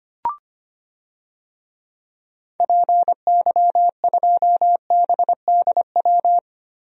Morse code, heard as a keyed tone, about 25 wpm, 700 Hz. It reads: PY2BDW
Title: DXCC-100 Call Signs - 6 Characters - Repeated 3x - 25wpm